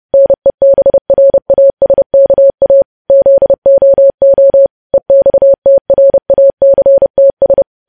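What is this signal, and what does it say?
Morse code, heard as a keyed tone, about 30 wpm, 550 Hz. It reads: NEBRASKA ZOO EXTRACTS
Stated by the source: Morse code news headlines sent at 30 words per minute.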